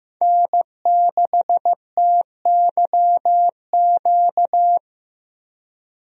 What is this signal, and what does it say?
Morse code, heard as a keyed tone, about 15 wpm, 700 Hz. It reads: N6TYQ
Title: USA Call Signs - 5 Characters - Repeated 2x - 15wpm